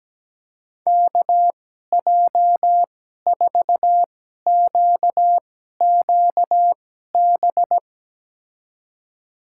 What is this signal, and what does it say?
Morse code, heard as a keyed tone, about 17 wpm, spaced as for 12 wpm, 700 Hz. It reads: KJ4QQB
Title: USA Call Signs - 6 Characters - 2x Character Spacing - 17wpm